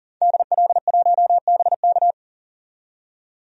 Morse code, heard as a keyed tone, about 40 wpm, 700 Hz. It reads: DL1BK
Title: DXCC-100 Call Signs - 5 Characters - 40wpm